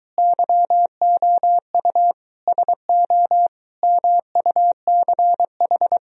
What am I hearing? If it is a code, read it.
Morse code, heard as a keyed tone, about 23 wpm, 700 Hz. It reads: YOU SO MUCH